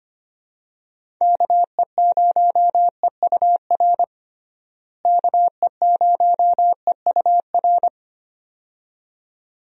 Morse code, heard as a keyed tone, about 25 wpm, 700 Hz. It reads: KE0EUR KE0EUR
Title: USA Call Signs - 6 Characters - Repeated 2x - 25wpm